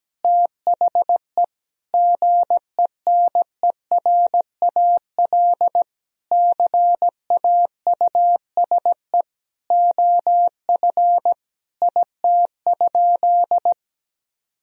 Morse code, heard as a keyed tone, about 17 wpm, 700 Hz. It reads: THE GENERAL CAUSE OF IT?
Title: Sentences from Top 1000 Words - Review Part 3 - 17wpm